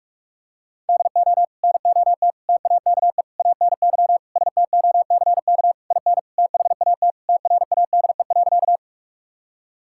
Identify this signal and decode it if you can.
Morse code, heard as a keyed tone, about 46 wpm, 700 Hz. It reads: DO NOT TAKE ANY STOCK IN THAT TRADE.
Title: Sentences from Top 1000 Words - Review Part 5 - 45wpm